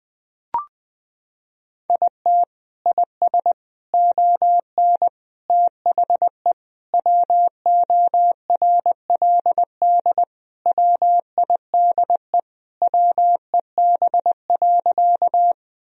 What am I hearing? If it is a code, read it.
Morse code, heard as a keyed tone, about 21 wpm, 700 Hz. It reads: IT IS ON THE WORLD WIDE WEB.
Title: Sentences from Top 1000 Words - Review Part 4 - 20wpm